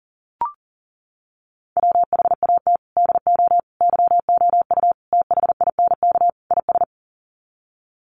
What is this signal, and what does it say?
Morse code, heard as a keyed tone, about 40 wpm, 700 Hz. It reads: WHAT DO YOU THINK IS
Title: Sets of 5 Words Encore 40wpm